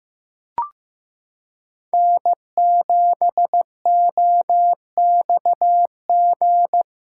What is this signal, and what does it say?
Morse code, heard as a keyed tone, about 15 wpm, 700 Hz. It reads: N7OXG
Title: USA Callsign DE Callsign - 5 Characters - 15wpm